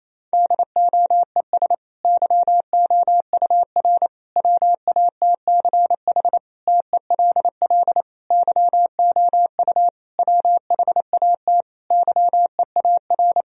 Morse code, heard as a keyed tone, about 28 wpm, 700 Hz. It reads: DOES YOUR WATCH TELL YOU WHAT YEAR
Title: Sentences from Top 1000 Words - Review Part 1 - 28wpm